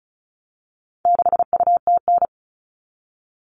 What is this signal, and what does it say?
Morse code, heard as a keyed tone, about 34 wpm, 700 Hz. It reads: 6UTN
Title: All Cut Numbers - 4-Digit 35wpm